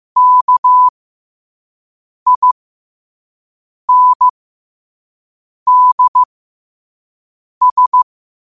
Morse code, heard as a keyed tone, about 15 wpm, spaced as for 5 wpm, 1000 Hz. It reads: KINDS
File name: qotdfiles / 5wpm